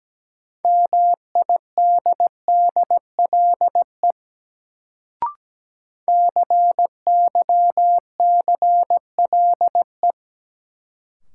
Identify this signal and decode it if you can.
Morse code, heard as a keyed tone, about 17 wpm, 700 Hz. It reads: MIDDLE CYCLE
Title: Sentences from Top 900 Words - New Word Drill - 17wpm